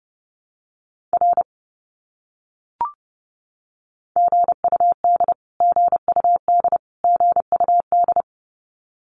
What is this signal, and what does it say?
Morse code, heard as a keyed tone, about 30 wpm, 700 Hz. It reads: R GUD GUD GUD
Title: Abbreviations Repeated 3x - 30wpm